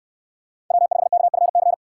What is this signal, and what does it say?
Morse code, heard as a keyed tone, about 62 wpm, 700 Hz. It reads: W5KFC